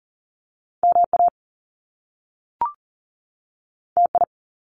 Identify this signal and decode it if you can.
Morse code, heard as a keyed tone, about 41 wpm, 700 Hz. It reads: MA TI
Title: Top Two Letter Combinations in Words 40wpm